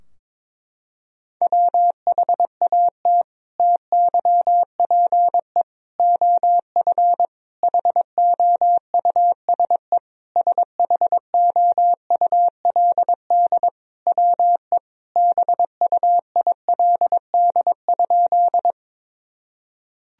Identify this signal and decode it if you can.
Morse code, heard as a keyed tone, about 22 wpm, 700 Hz. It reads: WHAT TYPE OF HOUSE SHOULD WE BUILD?
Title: Sentences from Top 1000 Words - Review Part 3 - 22wpm